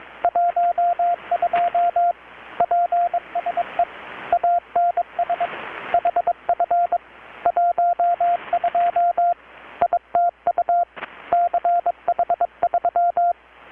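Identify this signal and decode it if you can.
Morse code, heard as a keyed tone, about 21 wpm, 700 Hz, with radio noise. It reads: 12 PSE ANS HF 12 ITU CH3
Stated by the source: Morse from KPH's Night of Nights XX maritime radiotelegraphy event, received in Chiapas, Mexico about 3,800 km away on a Kenwood R-600 with a grounded 10 m long-wire antenna.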